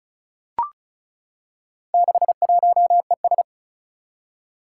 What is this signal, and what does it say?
Morse code, heard as a keyed tone, about 36 wpm, 700 Hz. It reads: 61ES
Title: Four Challenging Letter-Number Combinations 35wpm